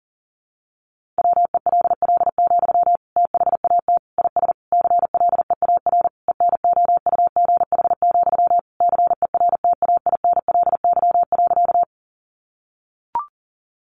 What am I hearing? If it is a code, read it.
Morse code, heard as a keyed tone, about 40 wpm, 700 Hz. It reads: WELL, THAT IS CLEAR ENOUGH, CERTAINLY.
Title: Sentences from Top 900 Words 40wpm